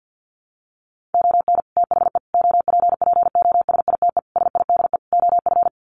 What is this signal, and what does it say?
Morse code, heard as a keyed tone, about 50 wpm, 700 Hz. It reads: ON THE OPPOSITE SIDE OF
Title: Sets of 5 Words Encore2 50wpm